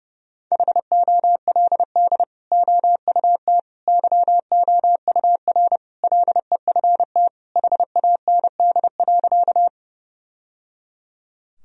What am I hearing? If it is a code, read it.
Morse code, heard as a keyed tone, about 30 wpm, 700 Hz. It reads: HOLD OUT YOUR LEFT HAND.
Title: Sentences from Top 1000 Words - Review Part 2 - 30wpm